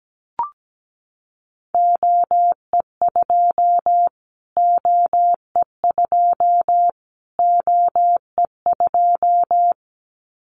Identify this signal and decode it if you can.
Morse code, heard as a keyed tone, about 17 wpm, 700 Hz. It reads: OE2 OE2 OE2
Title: Callsign Prefixes for Top 25 DXCC - Repeated 3x - 17wpm